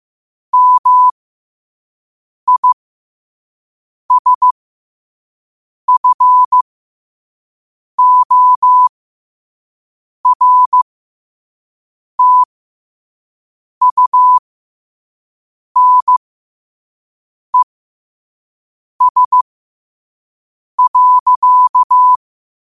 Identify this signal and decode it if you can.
Morse code, heard as a keyed tone, about 15 wpm, spaced as for 5 wpm, 1000 Hz. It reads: MISFORTUNES.
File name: qotdfiles / 5wpm